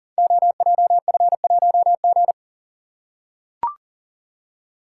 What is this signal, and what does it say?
Morse code, heard as a keyed tone, about 44 wpm, 690 Hz. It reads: OJF1G
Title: Code Groups - 5 Letter:Numbers - 40wpm